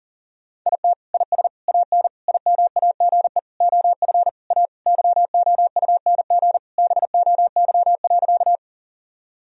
Morse code, heard as a keyed tone, about 40 wpm, 700 Hz. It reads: IT IS AN IMAGE OF A YOUNG BOY.
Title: Sentences from Top 1000 Words - Review Part 4 - 40wpm